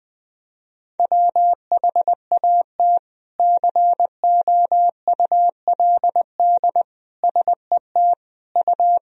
Morse code, heard as a keyed tone, about 20 wpm, 700 Hz. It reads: WHAT COULD SET U
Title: Sentences from Top 200 Words and Abbreviations 20wpm